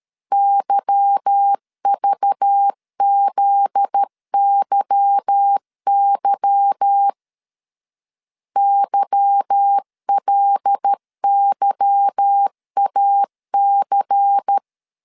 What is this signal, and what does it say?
Morse code, heard as a keyed tone, about 13 wpm, 800 Hz. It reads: YVZYY YLYAC
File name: LEZIONI / 8 YJ45 / 8 Y 12wpm 2228